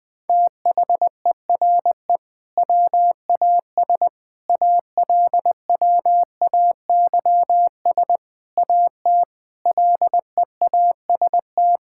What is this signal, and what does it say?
Morse code, heard as a keyed tone, about 21 wpm, 700 Hz. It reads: THERE WAS ALWAYS AT LEAST
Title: Sentences from Top 1600 Words 20wpm